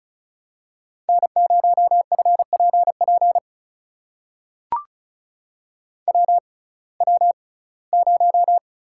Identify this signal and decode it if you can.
Morse code, heard as a keyed tone, about 36 wpm, spaced as for 21 wpm, 700 Hz. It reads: N0FPP WW0
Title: USA Call Signs - 6x Character Spacing - 35wpm